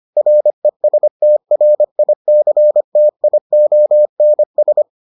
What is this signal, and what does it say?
Morse code, heard as a keyed tone, about 25 wpm, 600 Hz. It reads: RESTRICTIONS